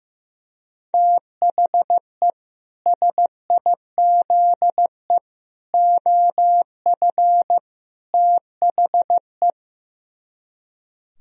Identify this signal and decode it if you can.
Morse code, heard as a keyed tone, about 15 wpm, 700 Hz. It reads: THE SIZE OF THE
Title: Speed-Racing - Sets of 4 Words 15wpm